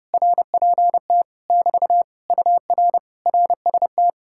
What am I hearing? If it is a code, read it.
Morse code, heard as a keyed tone, about 33 wpm, 700 Hz. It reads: RPT <BT> UR RST